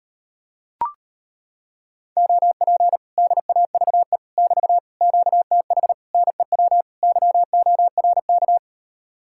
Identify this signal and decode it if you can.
Morse code, heard as a keyed tone, about 33 wpm, 700 Hz. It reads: OP DAVE <BT> QTH NEW YORK